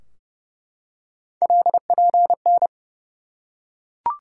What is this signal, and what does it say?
Morse code, heard as a keyed tone, about 32 wpm, 700 Hz. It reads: LPN
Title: Practice QSO - Exchange 3 - More Info - 30wpm